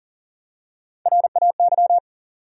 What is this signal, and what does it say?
Morse code, heard as a keyed tone, about 40 wpm, 700 Hz. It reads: RAY
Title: USA Common Names - Max 5 Letters - 40wpm